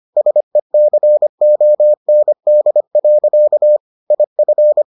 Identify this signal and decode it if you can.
Morse code, heard as a keyed tone, about 25 wpm, 600 Hz. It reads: SECOND. IF